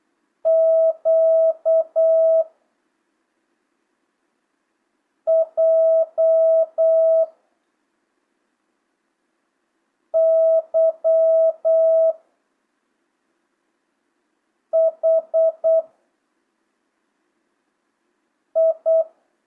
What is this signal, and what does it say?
Morse code, heard as a keyed tone, about 8 wpm, 650 Hz. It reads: Q J Y H I
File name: jh8cbh / 2025cw